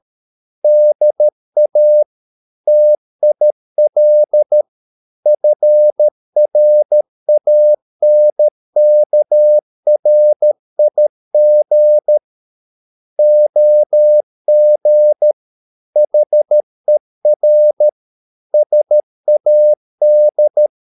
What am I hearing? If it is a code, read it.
Morse code, heard as a keyed tone, about 13 wpm, 600 Hz. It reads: DA TIL FRANKRIG OG HER SAD
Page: Ferie 13 wpm